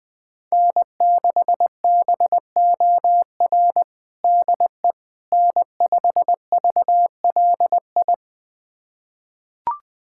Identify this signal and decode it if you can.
Morse code, heard as a keyed tone, about 21 wpm, 700 Hz. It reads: N6BOR DE N5VLI
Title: USA Callsign DE Callsign - 5 Characters - 20wpm